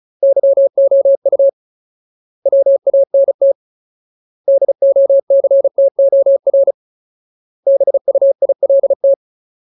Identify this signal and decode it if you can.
Morse code, heard as a keyed tone, about 34 wpm, 550 Hz. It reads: YOU WANT DOCTOR BUILT
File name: Ultimate Word List / 35WPM Ultimate